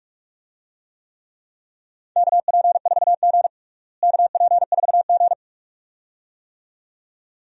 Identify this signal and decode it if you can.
Morse code, heard as a keyed tone, about 46 wpm, 700 Hz. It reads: KP4G KP4G
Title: DXCC-100 Call Signs - 4 Characters - Repeated 2x - 45wpm